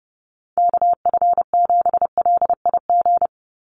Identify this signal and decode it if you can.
Morse code, heard as a keyed tone, about 30 wpm, 700 Hz. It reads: KF7LIG
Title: USA Call Signs 30wpm